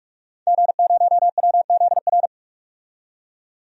Morse code, heard as a keyed tone, about 42 wpm, 700 Hz. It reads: G0WZR
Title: DXCC-100 Call Signs - 5 Characters - 45wpm